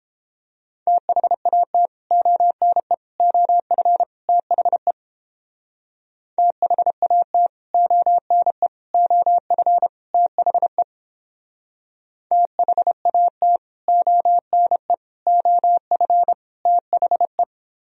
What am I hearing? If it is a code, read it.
Morse code, heard as a keyed tone, about 31 wpm, 700 Hz. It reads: THAT ONE OF THE THAT ONE OF THE THAT ONE OF THE